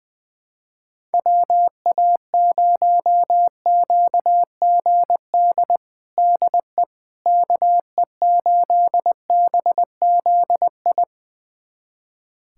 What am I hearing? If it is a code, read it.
Morse code, heard as a keyed tone, about 20 wpm, 700 Hz. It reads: WA0QGD DE KE8BZI
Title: USA Callsign DE Callsign 20wpm